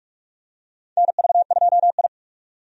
Morse code, heard as a keyed tone, about 46 wpm, 700 Hz. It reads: NV2I
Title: USA Call Signs - 4 Characters - 45wpm